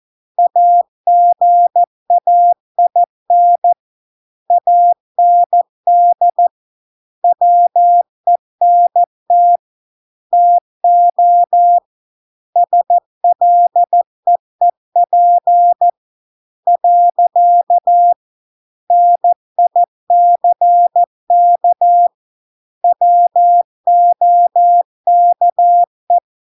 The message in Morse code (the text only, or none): AGAIN AND WENT TO SLEEP . NICK WOKE